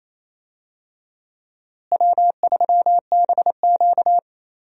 Morse code, heard as a keyed tone, about 28 wpm, 700 Hz. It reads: W3BQ
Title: USA Call Signs - 4 Characters - Repeated 2x - 28wpm